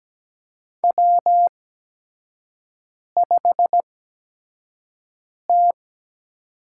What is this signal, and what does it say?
Morse code, heard as a keyed tone, about 17 wpm, spaced as for 5 wpm, 700 Hz. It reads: W5T
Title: USA Call Signs - 8x Character Spacing - 17wpm